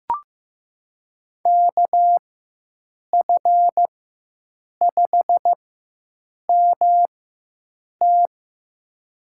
Morse code, heard as a keyed tone, about 16 wpm, spaced as for 7 wpm, 700 Hz. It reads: KF5MT
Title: USA Call Signs - 5 Characters - 4x Character Spacing - 15wpm